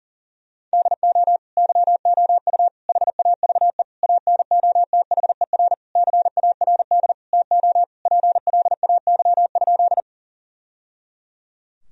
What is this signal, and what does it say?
Morse code, heard as a keyed tone, about 37 wpm, 700 Hz. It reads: DO YOU HAVE ANOTHER CARD TO PLAY?
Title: Sentences from Top 1000 Words - Review Part 4 - 40wpm